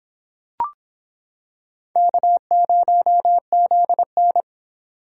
Morse code, heard as a keyed tone, about 25 wpm, 700 Hz. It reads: K0ZN